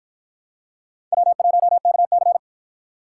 Morse code, heard as a keyed tone, about 53 wpm, 700 Hz. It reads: W1KC